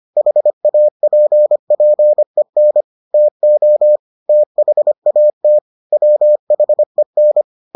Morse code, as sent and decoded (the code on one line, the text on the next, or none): .... .- .--. .--. -. / - --- / - .... .- - / .-- .... -.
HAPPEN TO THAT WHEN